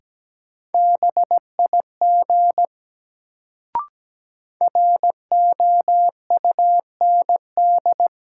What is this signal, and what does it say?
Morse code, heard as a keyed tone, about 17 wpm, 700 Hz. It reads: BIG ROUND
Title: Sentences from Top 200 Words - New Word Drill - 17wpm